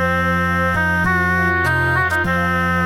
There is no code.